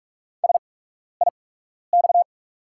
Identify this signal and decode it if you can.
Morse code, heard as a keyed tone, about 44 wpm, spaced as for 12 wpm, 690 Hz. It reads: SIX
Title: 8x Character Spacing - Top 500 Words - 45wpm